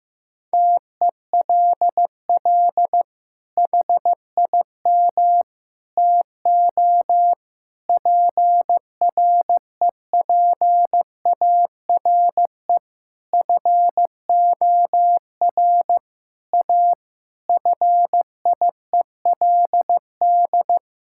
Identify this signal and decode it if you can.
Morse code, heard as a keyed tone, about 15 wpm, 700 Hz. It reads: TELL HIM TO PREPARE FOR A FIELD
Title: Sentences from Top 1000 Words - Review Part 6 - 15wpm